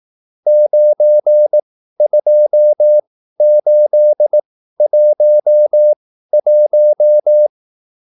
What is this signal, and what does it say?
Morse code, heard as a keyed tone, about 18 wpm, spaced as for 13 wpm, 600 Hz. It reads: 92811